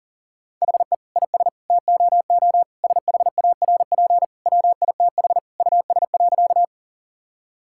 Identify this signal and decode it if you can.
Morse code, heard as a keyed tone, about 40 wpm, 700 Hz. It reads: HE IS TOO SHARP WITH US.
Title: Sentences from Top 1600 Words 40wpm